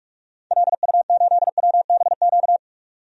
Morse code, heard as a keyed tone, about 47 wpm, 700 Hz. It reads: LU8WBQ
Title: DXCC-100 Call Signs - 6 Characters - 45wpm